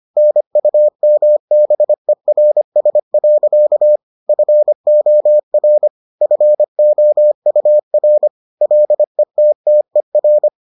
Morse code, heard as a keyed tone, about 25 wpm, 600 Hz. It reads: NUMBERS. FOR FOUR LETTER